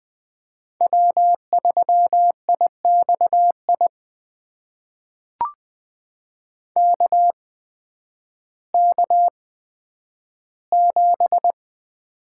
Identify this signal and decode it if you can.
Morse code, heard as a keyed tone, about 20 wpm, spaced as for 11 wpm, 700 Hz. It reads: W3IXI KK7
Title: USA Call Signs - 5 Characters - 8x Character Spacing - 20wpm